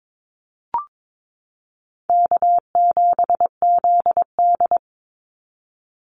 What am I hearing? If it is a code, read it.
Morse code, heard as a keyed tone, about 22 wpm, 700 Hz. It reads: K7ZD
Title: Easy Call Signs Repeated 2x - 22wpm